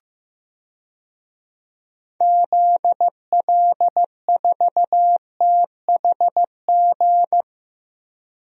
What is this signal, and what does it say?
Morse code, heard as a keyed tone, about 15 wpm, 700 Hz. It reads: ZL4THG
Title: DXCC-100 Call Signs - 6 Characters - 15wpm